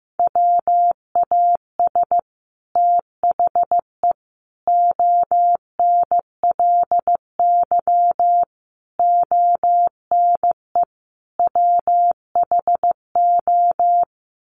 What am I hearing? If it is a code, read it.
Morse code, heard as a keyed tone, about 15 wpm, 700 Hz. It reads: WAS THE ONLY ONE WHO
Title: Sets of 5 Words Encore2 15wpm